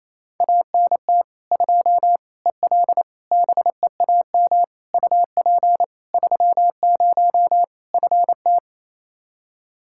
Morse code, heard as a keyed tone, about 28 wpm, 700 Hz. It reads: ANT 2 EL BEAM UP 30 FT
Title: Practice QSO - Exchange 2 - Info - 28wpm